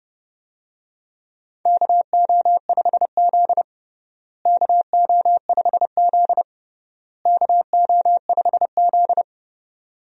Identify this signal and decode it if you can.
Morse code, heard as a keyed tone, about 31 wpm, 700 Hz. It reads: KO5Z KO5Z KO5Z
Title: USA Call Signs - 4 Characters - Repeated 3x - 30wpm